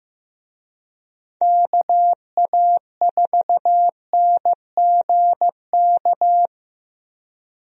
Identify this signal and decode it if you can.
Morse code, heard as a keyed tone, about 15 wpm, 700 Hz. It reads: KA4NGK
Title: USA Call Signs - 6 Characters - Repeated 3x - 15wpm